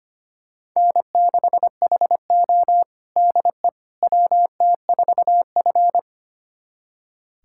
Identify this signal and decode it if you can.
Morse code, heard as a keyed tone, about 25 wpm, 700 Hz. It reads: N6HO DE WT4F
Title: USA Callsign DE Callsign - 4 Characters - 25wpm